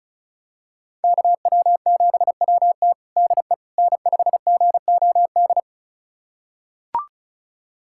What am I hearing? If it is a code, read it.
Morse code, heard as a keyed tone, about 36 wpm, 700 Hz. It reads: KW7WT DE N5GOD
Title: USA Callsign DE Callsign - 5 Characters - 35wpm